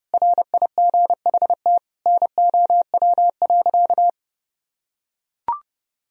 Morse code, heard as a keyed tone, about 30 wpm, 700 Hz. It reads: RIGHT NOW.
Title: Sentences from Top 1000 Words - Review Part 4 - 30wpm